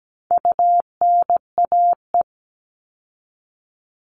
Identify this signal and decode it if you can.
Morse code, heard as a keyed tone, about 17 wpm, 700 Hz. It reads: UNAE
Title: All Cut Numbers - 4-Digit 17wpm